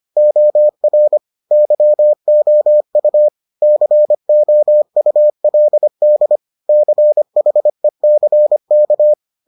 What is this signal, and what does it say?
Morse code, heard as a keyed tone, about 25 wpm, 600 Hz. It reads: OR YOU COULD CHECK